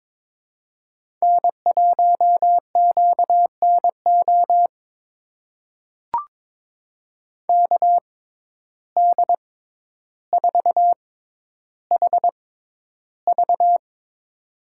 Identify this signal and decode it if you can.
Morse code, heard as a keyed tone, about 22 wpm, spaced as for 11 wpm, 700 Hz. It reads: N1QNO KD4HV
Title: USA Call Signs - 5 Characters - 6x Character Spacing - 22wpm